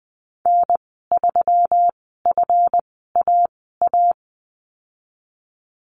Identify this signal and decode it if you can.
Morse code, heard as a keyed tone, about 19 wpm, spaced as for 14 wpm, 700 Hz. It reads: N3FAA
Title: 140 Lesson 23g - Letter f - Call Signs 20wpm